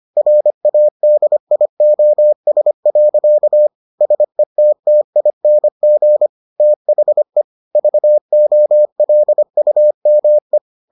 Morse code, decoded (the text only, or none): RADIOS. SETTING THE VOLUME